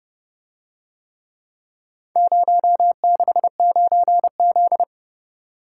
Morse code, heard as a keyed tone, about 30 wpm, 700 Hz. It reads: 069Z